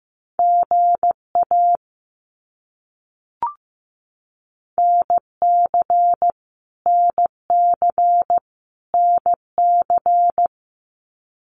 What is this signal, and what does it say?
Morse code, heard as a keyed tone, about 15 wpm, 700 Hz. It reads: GA NC NC NC
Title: US States Repeated 3x 15wpm